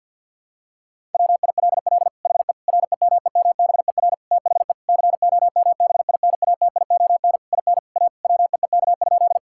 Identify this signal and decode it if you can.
Morse code, heard as a keyed tone, about 49 wpm, 700 Hz. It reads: WILL HE REMEMBER THE COMBINATION IN A WEEK?
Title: Sentences from Top 1000 Words - Review Part 6 - 50wpm